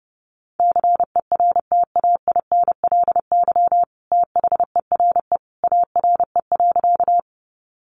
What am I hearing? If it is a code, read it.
Morse code, heard as a keyed tone, about 28 wpm, 700 Hz. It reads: CERTAINLY THERE ARE.
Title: Sentences from Top 900 Words 30wpm